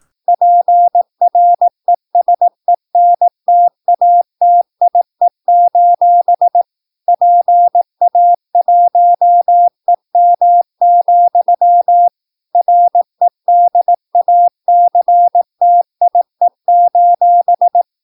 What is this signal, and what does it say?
Morse code, heard as a keyed tone, about 18 wpm, 700 Hz. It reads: PRESENTATIE: PA1EM, REDACTIE:
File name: PI4AA Uitzending